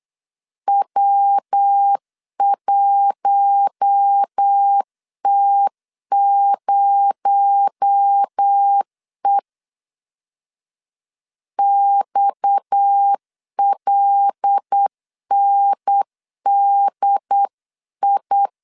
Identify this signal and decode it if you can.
Morse code, heard as a keyed tone, about 8 wpm, 800 Hz. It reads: W1T0E XLNDI
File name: 8wpm